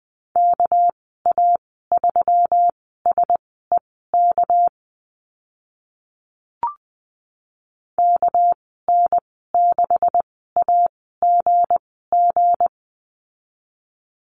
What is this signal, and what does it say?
Morse code, heard as a keyed tone, about 20 wpm, spaced as for 14 wpm, 700 Hz. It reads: KA3SEK KN6AGG
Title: 213 Lesson 34g - Number 8 - Call Signs 20wpm